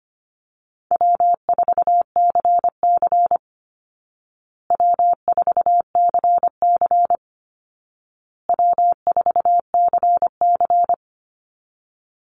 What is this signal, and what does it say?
Morse code, heard as a keyed tone, about 25 wpm, 700 Hz. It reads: W4CC W4CC W4CC
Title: Easy Call Signs Repeated 3x - 25wpm